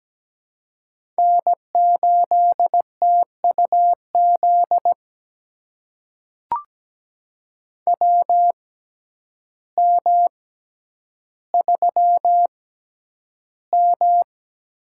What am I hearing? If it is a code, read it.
Morse code, heard as a keyed tone, about 17 wpm, spaced as for 9 wpm, 700 Hz. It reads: N8TUZ WM3M
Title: USA Call Signs - 6x Character Spacing - 17wpm